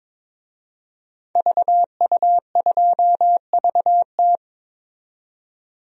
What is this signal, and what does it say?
Morse code, heard as a keyed tone, about 22 wpm, 700 Hz. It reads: VU2VT
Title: DXCC-100 Call Signs - 5 Characters - 22wpm